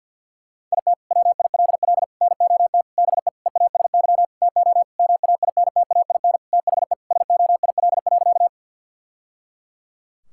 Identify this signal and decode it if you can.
Morse code, heard as a keyed tone, about 50 wpm, 700 Hz. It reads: IT WILL NOT BE EASY TO MAINTAIN THE SOIL.
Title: Sentences from Top 1000 Words - Review Part 5 - 50wpm